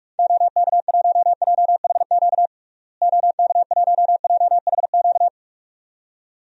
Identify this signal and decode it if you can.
Morse code, heard as a keyed tone, about 46 wpm, 700 Hz. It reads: OK1JHQ OK1JHQ
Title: DXCC-100 Call Signs - 6 Characters - Repeated 2x - 45wpm